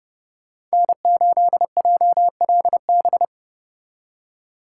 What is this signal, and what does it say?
Morse code, heard as a keyed tone, about 33 wpm, 700 Hz. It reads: N8JLB